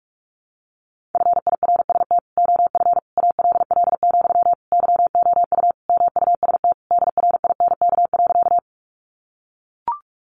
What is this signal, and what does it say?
Morse code, heard as a keyed tone, about 44 wpm, 700 Hz. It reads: FIRST OF ALL, YOU MUST DRINK.
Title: Sentences from Top 1500 Words 45wpm